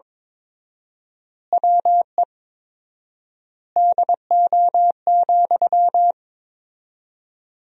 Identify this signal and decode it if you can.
Morse code, heard as a keyed tone, about 21 wpm, 700 Hz. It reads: WE DO,
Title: 4x Word-Spacing - Sentences from Top 100 Words - 22wpm